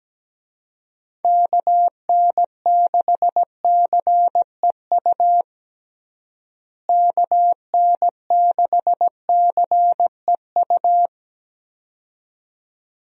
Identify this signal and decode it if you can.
Morse code, heard as a keyed tone, about 17 wpm, 700 Hz. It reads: KN6CEU KN6CEU
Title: USA Call Signs - 6 Characters - Repeated 2x - 17wpm